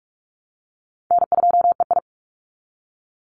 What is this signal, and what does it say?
Morse code, heard as a keyed tone, about 44 wpm, 700 Hz. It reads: N2EI
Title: Call Signs Repeated 2x - 45wpm